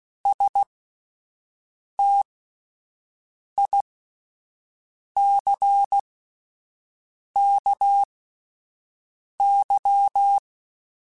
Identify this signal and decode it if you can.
Morse code, heard as a keyed tone, about 16 wpm, spaced as for 5 wpm, 800 Hz. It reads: STICKY
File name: Misc / Misc-mp3s / WordPairs sp @10